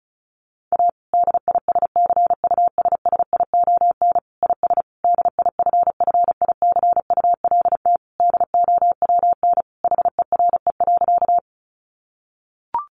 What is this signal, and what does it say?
Morse code, heard as a keyed tone, about 34 wpm, 700 Hz. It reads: A DISCUSSION IS DIFFICULT DOWN HERE.
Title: Sentences from Top 1100 Words 35wpm